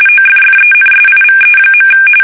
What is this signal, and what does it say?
RTTY, about 45.45 baud. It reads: OFFENSIVE.
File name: fsk850 / battleshipnj